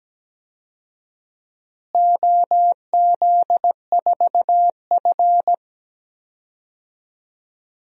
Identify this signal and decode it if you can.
Morse code, heard as a keyed tone, about 17 wpm, 700 Hz. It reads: OZ4F
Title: DXCC-100 Call Signs - 4 Characters - Repeated 3x - 17wpm